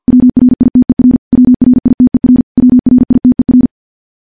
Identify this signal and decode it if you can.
Morse code, heard as a keyed tone, about 50 wpm, 250 Hz. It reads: WRITER WRITER WRITER
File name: morse-training-50WPM